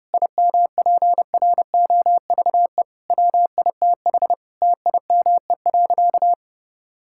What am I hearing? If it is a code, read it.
Morse code, heard as a keyed tone, about 30 wpm, 700 Hz. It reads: IMPROVE WITH TIME.
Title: Sentences from Top 1000 Words - Review Part 5 - 30wpm